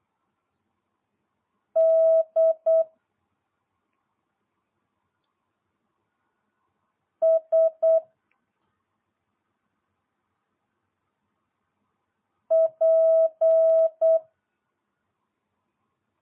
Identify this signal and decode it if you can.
Morse code, heard as a keyed tone, about 8 wpm, 650 Hz. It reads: D S P